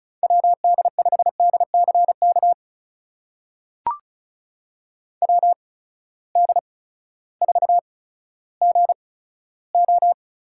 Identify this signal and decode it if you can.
Morse code, heard as a keyed tone, about 36 wpm, spaced as for 16 wpm, 700 Hz. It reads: WD5DCK WD4GO
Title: USA Call Signs - 6 Characters - 8x Character Spacing - 35wpm